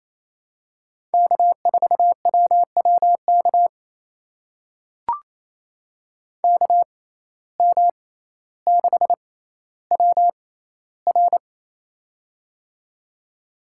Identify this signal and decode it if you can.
Morse code, heard as a keyed tone, about 28 wpm, spaced as for 14 wpm, 700 Hz. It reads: K4WWK KM6WR
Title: USA Call Signs - 5 Characters - 6x Character Spacing - 28wpm